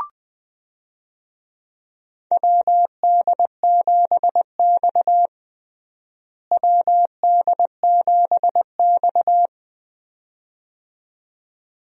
Morse code, heard as a keyed tone, about 21 wpm, 700 Hz. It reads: WD7X WD7X
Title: USA Call Signs - 4 Characters - Repeated 2x - 20wpm